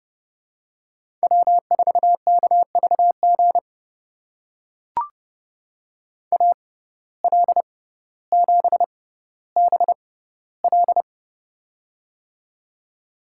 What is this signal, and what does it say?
Morse code, heard as a keyed tone, about 30 wpm, spaced as for 15 wpm, 700 Hz. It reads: W4KVG AL7BL
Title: USA Call Signs - 5 Characters - 6x Character Spacing - 30wpm